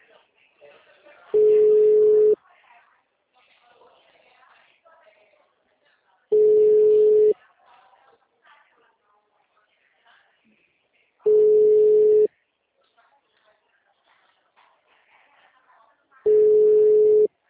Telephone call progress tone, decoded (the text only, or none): ringback tone